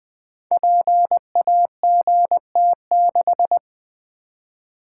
Morse code, heard as a keyed tone, about 20 wpm, 700 Hz. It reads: PAGT6
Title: Code Groups - 5 Letter:Numbers - 20wpm